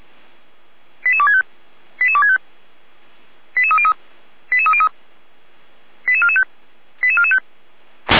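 Five-tone selective call (ZVEI), heard as quick runs of five tones, tones of about 70 ms. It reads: 88265 88265 88303 88303 88445 88445
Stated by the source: ZVEI five-tone selective call 88255 received via SDR.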